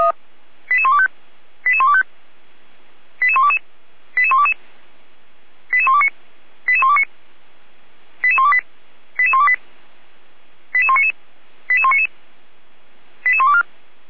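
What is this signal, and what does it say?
Five-tone selective call (ZVEI), heard as quick runs of five tones, tones of about 70 ms. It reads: 88125 88125 88122 88122 88129 88129 88128 88128 88199 88199 88124
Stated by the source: ZVEI five-tone selective call 88109, received by SDR.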